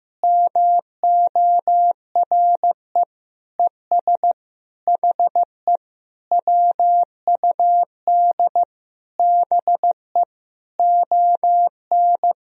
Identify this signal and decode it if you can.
Morse code, heard as a keyed tone, about 15 wpm, 700 Hz. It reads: MORE ES HE WUD BE ON
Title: Sentences from Top 300 Words and Abbreviations 15wpm